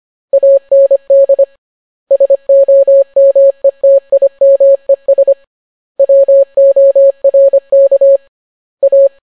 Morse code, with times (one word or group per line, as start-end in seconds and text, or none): 0.3-1.4 AND
2.1-5.3 SOMETIMES
6.0-8.2 WORK
8.8-9.1 A